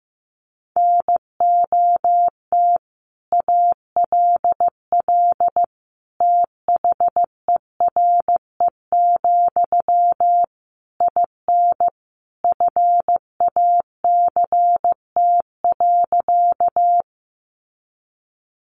Morse code, heard as a keyed tone, about 15 wpm, 700 Hz. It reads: NOT ALL THERE, IN FACT.
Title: Sentences from Top 600 Words 15wpm